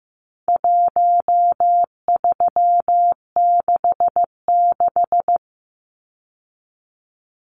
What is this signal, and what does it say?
Morse code, heard as a keyed tone, about 15 wpm, 700 Hz. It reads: 1366
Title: Numbers - Contest Style - 4-Digit 15wpm